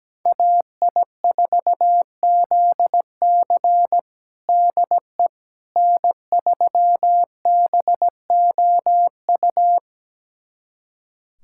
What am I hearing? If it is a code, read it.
Morse code, heard as a keyed tone, about 17 wpm, 700 Hz. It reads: AI4ZC DE N3BOU
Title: USA Callsign DE Callsign - 5 Characters - 17wpm